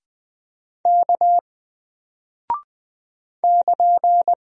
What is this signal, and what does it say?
Morse code, heard as a keyed tone, about 20 wpm, 700 Hz. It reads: K <KN>
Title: Single Character (Letter:Number:Punctuation:Prosign) Alt 20wpm